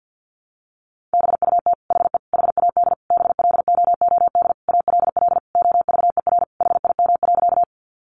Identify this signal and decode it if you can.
Morse code, heard as a keyed tone, about 48 wpm, 700 Hz. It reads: BUT HE HAD BLOOD ALL OVER HIM.
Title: Sentences from Top 1100 Words 50wpm